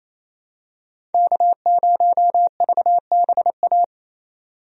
Morse code, heard as a keyed tone, about 28 wpm, 700 Hz. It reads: K0VBA
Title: USA Call Signs - 5 Characters - 28wpm